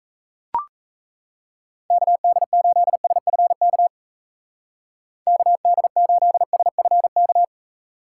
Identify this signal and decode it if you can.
Morse code, heard as a keyed tone, about 36 wpm, 700 Hz. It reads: KD8SFK KD8SFK